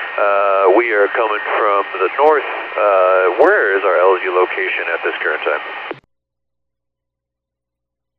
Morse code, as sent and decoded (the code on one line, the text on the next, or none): none